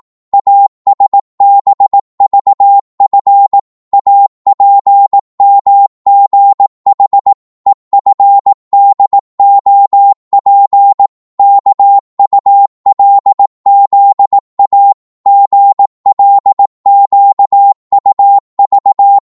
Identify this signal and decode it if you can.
Morse code, heard as a keyed tone, about 18 wpm, 800 Hz. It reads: ASBVF APMGH EFDOP KULZA GLQUV
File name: Morse / ltr-18wpm